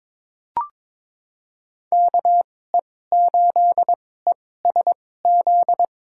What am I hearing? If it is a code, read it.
Morse code, heard as a keyed tone, about 22 wpm, spaced as for 16 wpm, 700 Hz. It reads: KE8ESZ
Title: USA Call Signs - 2x Character Spacing - 22wpm